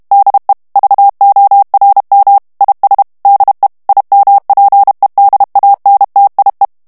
Morse code, heard as a keyed tone, about 32 wpm, 800 Hz. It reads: DE VORM IS DE IMPEDANTIE
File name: HOLLANDAIS texte clair 1 vitesse 30 mots minute